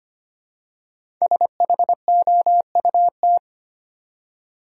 Morse code, heard as a keyed tone, about 25 wpm, 700 Hz. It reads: SHOUT
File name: 5 Letter Words - First 3 Letters Given - 25wpm